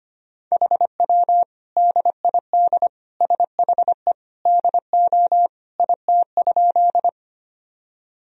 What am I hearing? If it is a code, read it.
Morse code, heard as a keyed tone, about 25 wpm, 700 Hz. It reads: HW DID SHE DO IT?
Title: Sentences from Top 100 Words and Abbreviations 25wpm